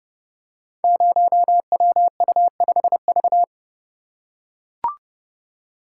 Morse code, heard as a keyed tone, about 30 wpm, 700 Hz. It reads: 0WU5V
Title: Code Groups - 5 Letter:Numbers - 30wpm